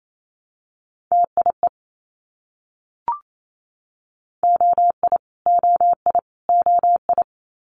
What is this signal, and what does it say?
Morse code, heard as a keyed tone, about 28 wpm, 700 Hz. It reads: TIE OI OI OI
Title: Multi-letter Phonemes Repeated 3x 28wpm